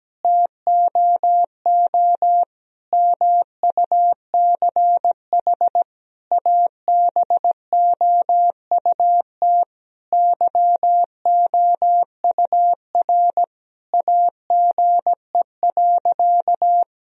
Morse code, heard as a keyed tone, about 17 wpm, 700 Hz. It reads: TOO MUCH ABOUT YOUR AGE.
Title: Sentences from Top 1000 Words - Review Part 3 - 17wpm